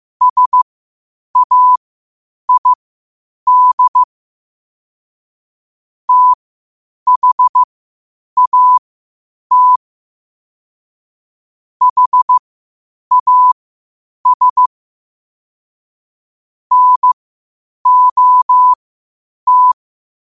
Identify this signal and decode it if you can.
Morse code, heard as a keyed tone, about 15 wpm, spaced as for 8 wpm, 1000 Hz. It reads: SAID THAT HAS NOT
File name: qotdfiles / 8wpm